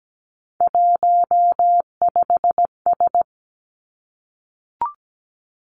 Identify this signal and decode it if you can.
Morse code, heard as a keyed tone, about 17 wpm, 700 Hz. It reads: 15S
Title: Three Challenging Letter-Number Combinations 17wpm